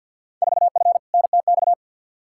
Morse code, heard as a keyed tone, about 49 wpm, 700 Hz. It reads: 4F NTX
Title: Field Day Exchanges Encore - 50wpm